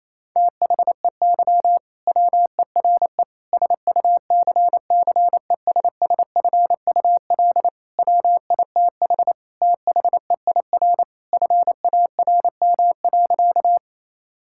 Morse code, heard as a keyed tone, about 28 wpm, 700 Hz. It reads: THEY WERE SUCCESSFUL WITH THEIR FARM.
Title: Sentences from Top 1000 Words - Review Part 6 - 28wpm